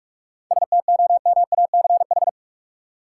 Morse code, heard as a keyed tone, about 42 wpm, 700 Hz. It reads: STOMACH
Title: Seven Letter Words - 45wpm